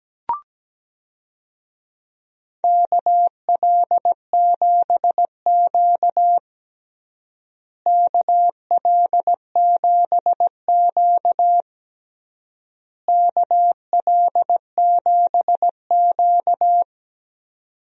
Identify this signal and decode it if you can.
Morse code, heard as a keyed tone, about 17 wpm, 700 Hz. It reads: KL7Q KL7Q KL7Q
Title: DXCC-100 Call Signs - 4 Characters - Repeated 3x - 17wpm